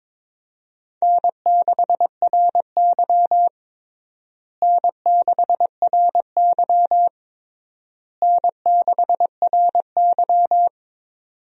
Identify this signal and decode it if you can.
Morse code, heard as a keyed tone, about 22 wpm, 700 Hz. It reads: N6RY N6RY N6RY
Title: USA Call Signs - 4 Characters - Repeated 3x - 22wpm